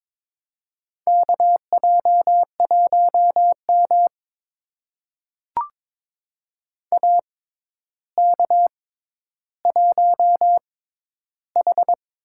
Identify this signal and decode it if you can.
Morse code, heard as a keyed tone, about 21 wpm, spaced as for 11 wpm, 700 Hz. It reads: KJ1M AK1H
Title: USA Call Signs - 4 Characters - 6x Character Spacing - 22wpm